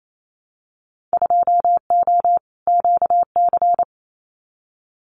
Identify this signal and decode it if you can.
Morse code, heard as a keyed tone, about 28 wpm, 700 Hz. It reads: 2O QC
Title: Winter Field Day Exchanges 28wpm